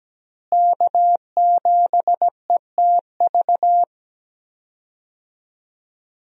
Morse code, heard as a keyed tone, about 17 wpm, 700 Hz. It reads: K7ETV
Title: USA Call Signs - 5 Characters - Repeated 3x - 17wpm